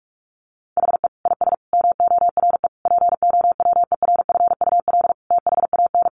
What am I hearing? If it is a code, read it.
Morse code, heard as a keyed tone, about 44 wpm, 700 Hz. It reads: HE IS MORE POWERFUL THAN
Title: Sentences from Top 1300 Words 45wpm